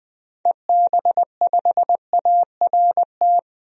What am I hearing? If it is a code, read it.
Morse code, heard as a keyed tone, about 20 wpm, 700 Hz. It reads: EB5ART
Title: DXCC-100 Call Signs - 6 Characters - 20wpm